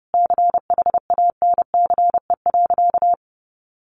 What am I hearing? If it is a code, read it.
Morse code, heard as a keyed tone, about 30 wpm, 700 Hz. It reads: CHANCE.
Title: Sentences from Top 1100 Words 30wpm